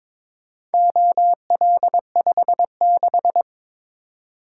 Morse code, heard as a keyed tone, about 22 wpm, 700 Hz. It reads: OL56
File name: practice / 22wpm / Four Letter-Number Combinations - 22wpm